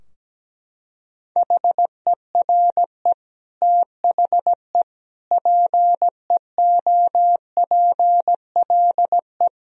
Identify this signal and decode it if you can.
Morse code, heard as a keyed tone, about 17 wpm, 700 Hz. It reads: HERE THE PEOPLE
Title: Sentences from Top 1000 Words - Review Part 2 - 17wpm